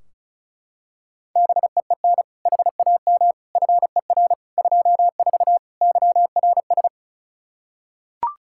Morse code, heard as a keyed tone, about 36 wpm, 700 Hz. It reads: BEEN HAM FER 24 YRS
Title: Practice QSO - Exchange 3 - More Info - 35wpm